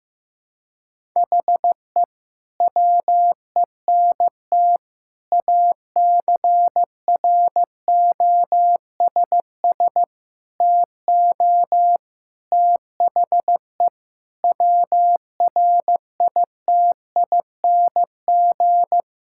Morse code, heard as a keyed tone, about 15 wpm, 700 Hz. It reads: HE WENT ACROSS TO THE WRITING
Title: Sentences from Top 1000 Words - Review Part 5 - 15wpm